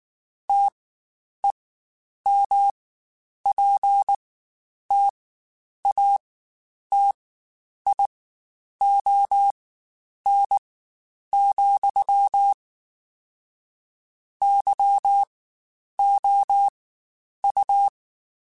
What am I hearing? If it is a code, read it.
Morse code, heard as a keyed tone, about 19 wpm, spaced as for 9 wpm, 800 Hz. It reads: TEMPTATION, YOU
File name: Spaced / Chuckles Spaced-mp3s / Temptation sp @18